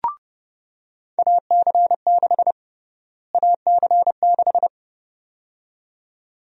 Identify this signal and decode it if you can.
Morse code, heard as a keyed tone, about 30 wpm, 700 Hz. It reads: AC6 AC6
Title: USA Call Sign Prefixes - Repeated 2x - 30wpm